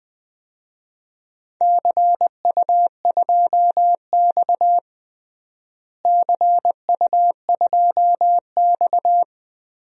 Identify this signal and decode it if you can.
Morse code, heard as a keyed tone, about 20 wpm, 700 Hz. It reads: CU2X CU2X
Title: DXCC-100 Call Signs - 4 Characters - Repeated 3x - 20wpm